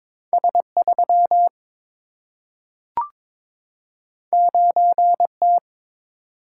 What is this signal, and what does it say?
Morse code, heard as a keyed tone, about 22 wpm, 700 Hz. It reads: S 3 9 T
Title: Single-Letter Single-Number 22wpm